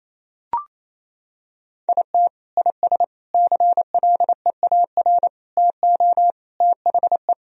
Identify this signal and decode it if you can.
Morse code, heard as a keyed tone, about 27 wpm, 700 Hz. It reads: IT IS CLEAR TO THE
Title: Sentences from Top 1000 Words - Review Part 2 - 28wpm